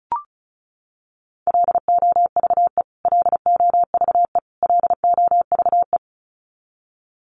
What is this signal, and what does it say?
Morse code, heard as a keyed tone, about 34 wpm, 700 Hz. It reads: LOVE LOVE LOVE
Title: Top 500 Words Repeated 35wpm